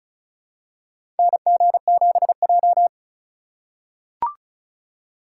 Morse code, heard as a keyed tone, about 36 wpm, 700 Hz. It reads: NG7J
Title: USA Call Signs - 4 Characters - 3x Character Spacing - 35wpm